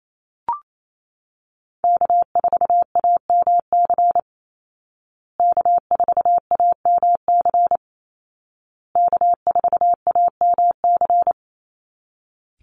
Morse code, heard as a keyed tone, about 28 wpm, 700 Hz. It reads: K4AMC K4AMC K4AMC
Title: Call Signs Repeated 2x - 28wpm